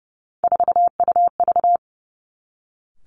Morse code, heard as a keyed tone, about 30 wpm, 700 Hz. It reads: 4UV
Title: Three Challenging Letter-Number Combinations 30wpm